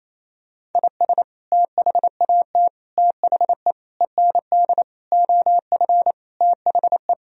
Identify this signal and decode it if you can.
Morse code, heard as a keyed tone, about 28 wpm, 700 Hz. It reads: IS THAT THE END OF THE
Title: Sentences from Top 1000 Words - Review Part 1 - 28wpm